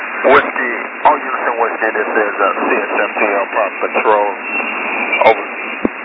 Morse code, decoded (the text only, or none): none